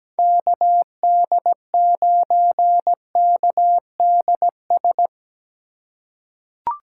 Morse code, heard as a keyed tone, about 17 wpm, 700 Hz. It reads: KD9KDS
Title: USA Call Signs - 4x Character Spacing - 17wpm